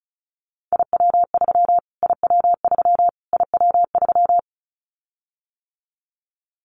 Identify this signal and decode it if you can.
Morse code, heard as a keyed tone, about 34 wpm, 700 Hz. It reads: IW3 IW3 IW3
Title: Callsign Prefixes for Top 25 DXCC - Repeated 3x - 35wpm